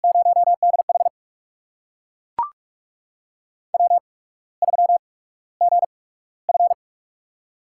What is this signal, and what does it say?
Morse code, heard as a keyed tone, about 42 wpm, 700 Hz. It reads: 0 D H W 3 G F
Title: USA Call Signs - 4 Characters - 8x Character Spacing - 45wpm